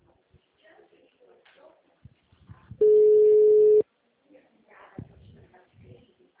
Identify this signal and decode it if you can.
Ringback tone (Europe, 425 Hz).
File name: out- -2017- - - .26561